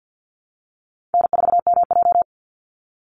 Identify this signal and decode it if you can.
Morse code, heard as a keyed tone, about 50 wpm, 700 Hz. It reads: N4MJ
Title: NA SOTA Chaser Callsigns 50wpm